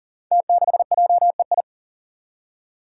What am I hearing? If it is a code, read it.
Morse code, heard as a keyed tone, about 40 wpm, 700 Hz. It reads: T6JEI
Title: Five Challenging Letter-Number Combinations 40wpm